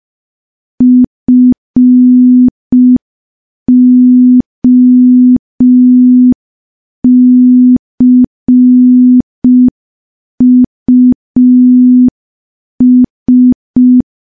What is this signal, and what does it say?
Morse code, heard as a keyed tone, about 5 wpm, 250 Hz. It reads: FOCUS
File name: morse / morse-training-05WPM